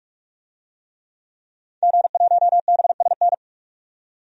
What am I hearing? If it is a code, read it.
Morse code, heard as a keyed tone, about 41 wpm, 700 Hz. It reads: G1BSN
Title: DXCC-100 Call Signs - 5 Characters - 45wpm